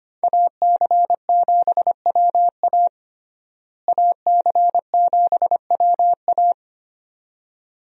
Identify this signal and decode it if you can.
Morse code, heard as a keyed tone, about 25 wpm, 700 Hz. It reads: AC7WA AC7WA
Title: USA Call Signs - 5 Characters - Repeated 2x - 25wpm